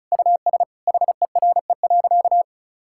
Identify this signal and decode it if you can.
Morse code, heard as a keyed tone, about 34 wpm, 700 Hz. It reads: US HERE.
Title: Sentences from Top 1000 Words - Review Part 5 - 35wpm